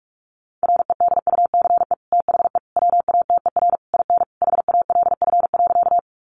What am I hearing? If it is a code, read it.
Morse code, heard as a keyed tone, about 44 wpm, 700 Hz. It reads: REDUCE THE WATER IN HALF.
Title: Sentences from Top 1100 Words 45wpm